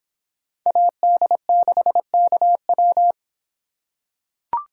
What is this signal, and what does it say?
Morse code, heard as a keyed tone, about 26 wpm, 700 Hz. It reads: AD6KW